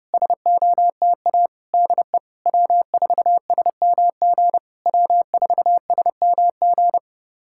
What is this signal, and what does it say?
Morse code, heard as a keyed tone, about 30 wpm, 700 Hz. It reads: SOTA DE W4SMG W4SMG
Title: SOTA QSO Practice 30wpm